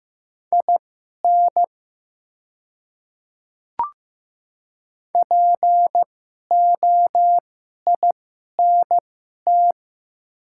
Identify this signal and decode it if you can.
Morse code, heard as a keyed tone, about 15 wpm, spaced as for 11 wpm, 700 Hz. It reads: IN POINT
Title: 2x Character Spacing - Top 200 Words - 15wpm